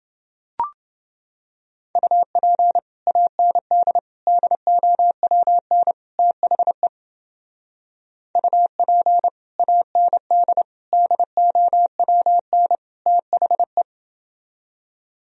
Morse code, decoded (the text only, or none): UP AND DOWN THE UP AND DOWN THE